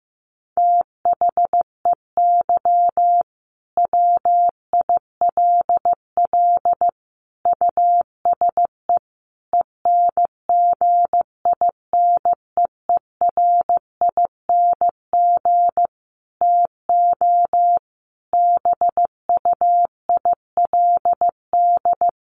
Morse code, heard as a keyed tone, about 15 wpm, 700 Hz. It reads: THEY WILL USE ENGINEERING TO BUILD
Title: Sentences from Top 1500 Words 15wpm